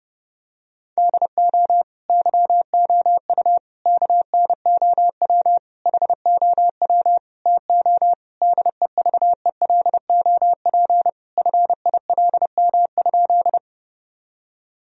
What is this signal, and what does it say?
Morse code, heard as a keyed tone, about 30 wpm, 700 Hz. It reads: DO YOU KNOW HOW TO DEVELOP FILM?
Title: Sentences from Top 1000 Words - Review Part 3 - 30wpm